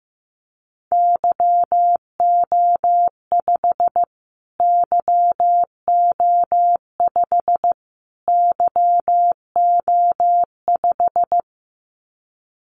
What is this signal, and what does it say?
Morse code, heard as a keyed tone, about 15 wpm, 700 Hz. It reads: YO5 YO5 YO5
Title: Callsign Prefixes for Top 25 DXCC - Repeated 3x - 15wpm